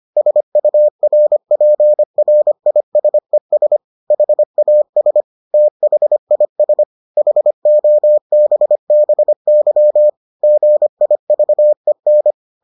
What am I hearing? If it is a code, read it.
Morse code, heard as a keyed tone, about 25 wpm, 600 Hz. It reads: SURPRISES HAS THIS HOBBY GIVEN